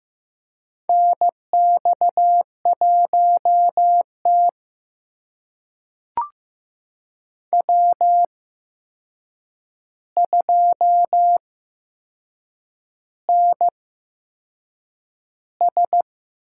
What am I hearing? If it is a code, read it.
Morse code, heard as a keyed tone, about 15 wpm, spaced as for 6 wpm, 700 Hz. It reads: NX1T W2NS
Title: USA Call Signs - 4 Characters - 8x Character Spacing - 15wpm